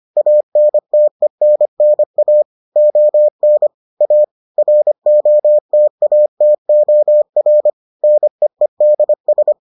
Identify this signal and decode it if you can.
Morse code, heard as a keyed tone, about 25 wpm, 600 Hz. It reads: ANTENNA ON A ROTATOR NEEDS